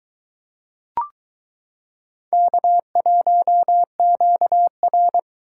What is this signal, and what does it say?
Morse code, heard as a keyed tone, about 22 wpm, 700 Hz. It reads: K1QR